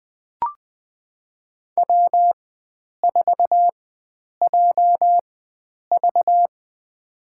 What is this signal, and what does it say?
Morse code, heard as a keyed tone, about 20 wpm, spaced as for 9 wpm, 700 Hz. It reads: W4JV
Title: USA Call Signs - 4 Characters - 4x Character Spacing - 20wpm